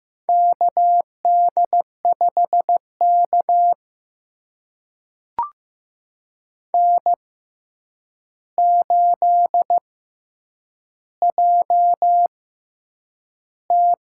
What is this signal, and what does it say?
Morse code, heard as a keyed tone, about 15 wpm, spaced as for 8 wpm, 700 Hz. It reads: KD5K N8JT
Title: USA Call Signs - 4 Characters - 6x Character Spacing - 15wpm